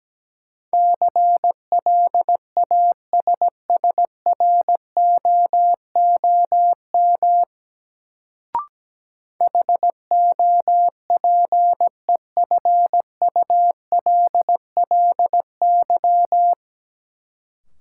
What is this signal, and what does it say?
Morse code, heard as a keyed tone, about 17 wpm, 700 Hz. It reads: CLASSROOM HOPEFULLY
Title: Sentences from Top 1500 Words - New Word Drill - 17wpm